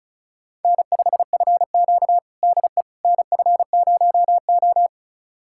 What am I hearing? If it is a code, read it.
Morse code, heard as a keyed tone, about 36 wpm, 700 Hz. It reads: N5FQ DE NF0O
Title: USA Callsign DE Callsign - 4 Characters - 35wpm